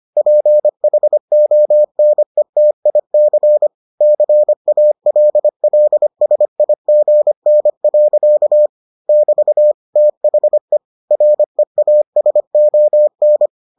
Morse code, heard as a keyed tone, about 25 wpm, 600 Hz. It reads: PHONETIC CALLSIGN. <BT> THE REASON